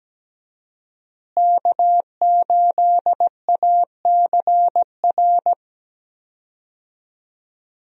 Morse code, heard as a keyed tone, about 17 wpm, 700 Hz. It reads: K8ACR
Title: USA Call Signs - 5 Characters - Repeated 2x - 17wpm